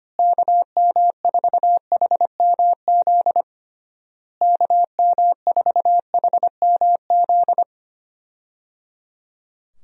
Morse code, heard as a keyed tone, about 25 wpm, 700 Hz. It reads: KM4HMZ KM4HMZ
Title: USA Call Signs - 6 Characters - Repeated 3x - 25wpm